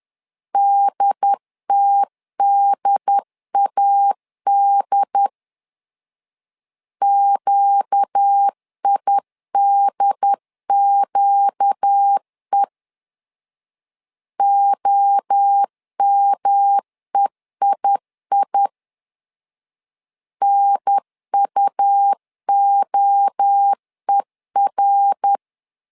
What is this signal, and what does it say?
Morse code, heard as a keyed tone, about 10 wpm, 800 Hz. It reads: DTDAD QIDQE OMEII NUOER
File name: LEZIONI / 4 Q 10wpm 2210